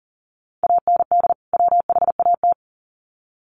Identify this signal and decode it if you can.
Morse code, heard as a keyed tone, about 40 wpm, 700 Hz. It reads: AND WHAT
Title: Sets of 2 Words 40wpm